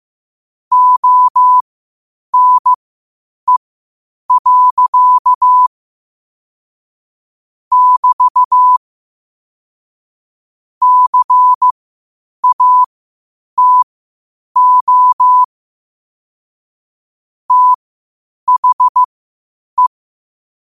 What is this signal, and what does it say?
Morse code, heard as a keyed tone, about 15 wpm, spaced as for 8 wpm, 1000 Hz. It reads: ONE. <BT> CATO THE